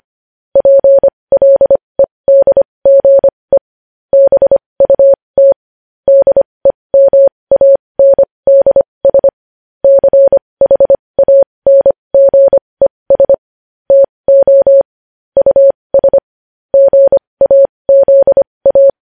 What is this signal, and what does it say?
Morse code, heard as a keyed tone, about 25 wpm, spaced as for 20 wpm, 550 Hz. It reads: PLEDGE BUT DEMANDS CHANGES TO US GAZA
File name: podcast / News headlines-1x-25x20